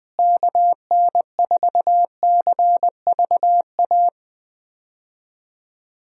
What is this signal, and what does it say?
Morse code, heard as a keyed tone, about 21 wpm, 700 Hz. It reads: KN4CVA
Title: USA Call Signs - 6 Characters - Repeated 2x - 20wpm